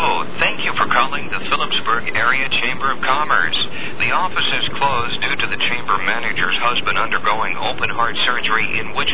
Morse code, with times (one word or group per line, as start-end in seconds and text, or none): none